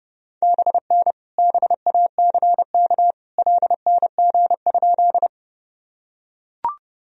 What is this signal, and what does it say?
Morse code, heard as a keyed tone, about 30 wpm, 700 Hz. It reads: BN BACK LNG?
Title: Sentences from Top 200 Words and Abbreviations 30wpm